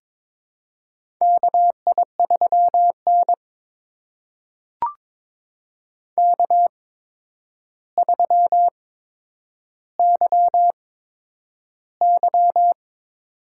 Morse code, heard as a keyed tone, about 22 wpm, spaced as for 9 wpm, 700 Hz. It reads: KI3N K3YY
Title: USA Call Signs - 4 Characters - 8x Character Spacing - 22wpm